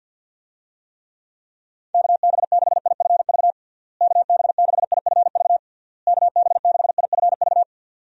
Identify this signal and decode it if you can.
Morse code, heard as a keyed tone, about 52 wpm, 700 Hz. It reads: KB6IFV KB6IFV KB6IFV
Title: USA Call Signs - 6 Characters - Repeated 3x - 50wpm